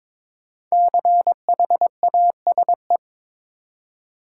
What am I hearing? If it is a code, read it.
Morse code, heard as a keyed tone, about 22 wpm, 700 Hz. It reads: CHASE
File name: 5 Letter Words - First 3 Letters Given - 22wpm